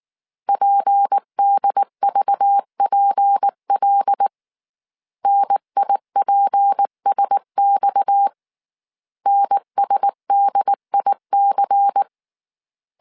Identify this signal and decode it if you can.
Morse code, heard as a keyed tone, about 19 wpm, 800 Hz. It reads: PDVPL NIPSX NSDIC
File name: CW / DET / 1 6 1500C S C 18wpm 2356